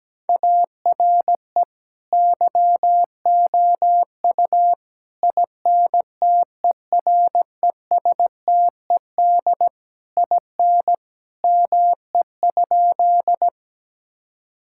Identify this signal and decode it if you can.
Morse code, heard as a keyed tone, about 17 wpm, 700 Hz. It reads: ARE YOU INTERESTED IN ME?
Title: Sentences from Top 1000 Words - Review Part 6 - 17wpm